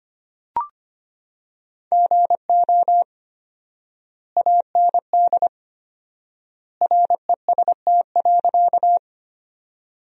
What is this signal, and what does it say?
Morse code, heard as a keyed tone, about 25 wpm, 700 Hz. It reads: GO AND REST.
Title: 4x Word-Spacing - Sentences from Top 500 Words - 25wpm